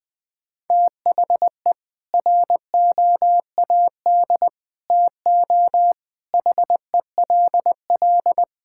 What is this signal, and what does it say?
Morse code, heard as a keyed tone, about 19 wpm, 700 Hz. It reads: THE ROAD TO HELL
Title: Sentences with 8-Word Phrases - 20wpm